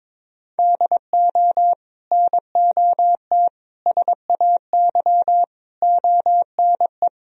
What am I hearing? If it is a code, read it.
Morse code, heard as a keyed tone, about 22 wpm, 700 Hz. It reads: DO NOT SAY ONE
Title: Sentences from Top 1000 Words - Review Part 4 - 22wpm